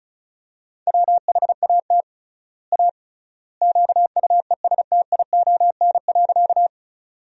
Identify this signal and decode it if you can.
Morse code, heard as a keyed tone, about 36 wpm, 700 Hz. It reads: WHAT A QUESTION.
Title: 3x Word-Spacing - Sentences from Top 400 Words - 35wpm